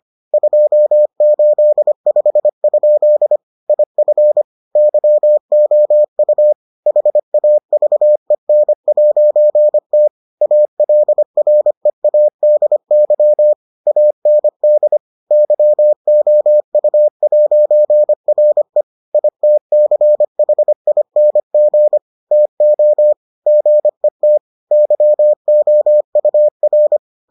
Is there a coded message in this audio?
Yes: Morse code from 0.3 to 27.0 s, heard as a keyed tone, about 25 wpm, 600 Hz. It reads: 285? IF YOU HAVEN'T ALREADY AND YOU'RE ITCHING TO GET YOUR